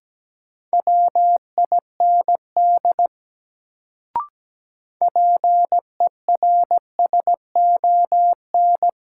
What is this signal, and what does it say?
Morse code, heard as a keyed tone, about 17 wpm, 700 Hz. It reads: WIND PERSON
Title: Sentences from Top 400 Words - New Word Drill - 17wpm